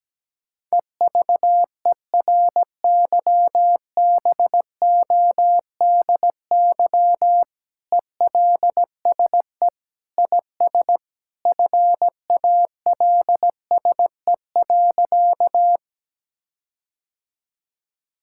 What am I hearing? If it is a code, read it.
Morse code, heard as a keyed tone, about 17 wpm, 700 Hz. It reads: EVERYBODY ELSE IS FALSE.
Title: Sentences from Top 1600 Words 17wpm